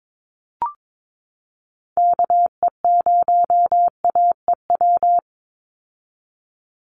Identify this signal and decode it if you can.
Morse code, heard as a keyed tone, about 22 wpm, 700 Hz. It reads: KE0AEW
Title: USA Call Signs 22wpm